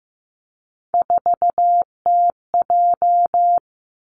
Morse code, heard as a keyed tone, about 15 wpm, 700 Hz. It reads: 4TJ
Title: Three Challenging Letter-Number Combinations 15wpm